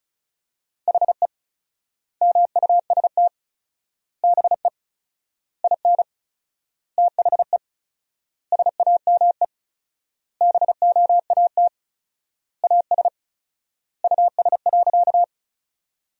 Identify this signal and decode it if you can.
Morse code, heard as a keyed tone, about 36 wpm, 700 Hz. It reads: HE MUST BE IN THE SAME BOAT AS US.
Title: 4x Word-Spacing - Sentences from Top 500 Words - 35wpm